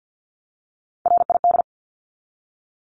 Morse code, heard as a keyed tone, about 50 wpm, 700 Hz. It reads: RID
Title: Top 1751-2000 Words 50wpm